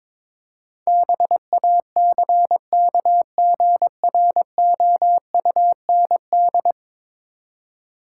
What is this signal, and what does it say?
Morse code, heard as a keyed tone, about 22 wpm, 700 Hz. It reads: BACKGROUND
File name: practice / 22wpm / 10 Letter Words - First 3 Letters Given - 22wpm